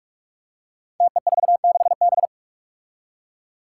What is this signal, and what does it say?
Morse code, heard as a keyed tone, about 46 wpm, 700 Hz. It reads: TE46B
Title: Five Challenging Letter-Number Combinations 45wpm